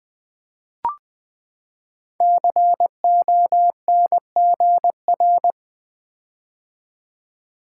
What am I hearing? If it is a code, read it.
Morse code, heard as a keyed tone, about 20 wpm, 700 Hz. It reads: CONGR
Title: Build A Word - 5 Letters At A Time - 12-Letter Words - 20wpm